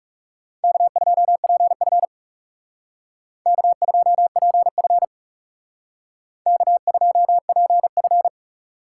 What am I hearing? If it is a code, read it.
Morse code, heard as a keyed tone, about 38 wpm, 700 Hz. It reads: K2PF K2PF K2PF